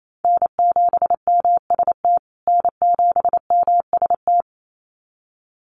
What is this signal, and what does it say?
Morse code, heard as a keyed tone, about 28 wpm, 700 Hz. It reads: N7MST N7MST
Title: USA Call Signs Repeated 2x 28wpm